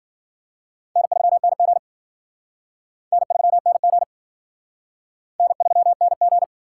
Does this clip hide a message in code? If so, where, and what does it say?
Yes: Morse code from 1.0 to 6.4 s, heard as a keyed tone, about 51 wpm, 700 Hz. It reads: N3NG N3NG N3NG